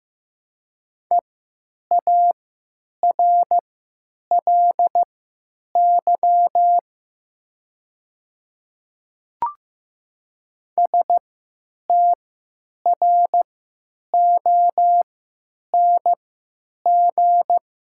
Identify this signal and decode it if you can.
Morse code, heard as a keyed tone, about 15 wpm, spaced as for 9 wpm, 700 Hz. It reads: EARLY STRONG
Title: 3x Character Spacing - Top 500 Words - 15wpm